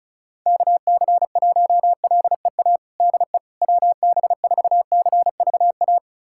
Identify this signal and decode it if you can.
Morse code, heard as a keyed tone, about 36 wpm, 700 Hz. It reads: KC1LEA DE WB4CVA
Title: USA Callsign DE Callsign 35wpm